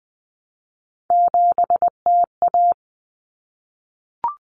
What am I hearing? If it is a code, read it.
Morse code, heard as a keyed tone, about 20 wpm, 700 Hz. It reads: 7TA
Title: Common Cut Numbers - 3-Digit 20wpm